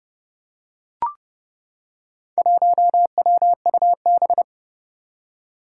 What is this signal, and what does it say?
Morse code, heard as a keyed tone, about 30 wpm, 700 Hz. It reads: 1WUB
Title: Four Challenging Letter-Number Combinations 30wpm